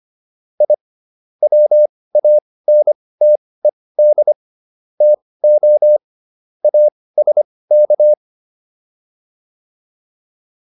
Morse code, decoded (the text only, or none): I WANTED TO ASK